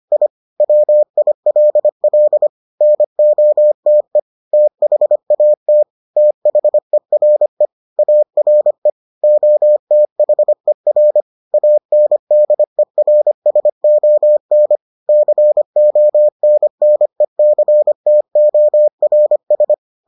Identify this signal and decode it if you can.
Morse code, heard as a keyed tone, about 25 wpm, 600 Hz. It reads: I WILL NOTE THAT THERE ARE OTHER ANDERSON CONNECTORS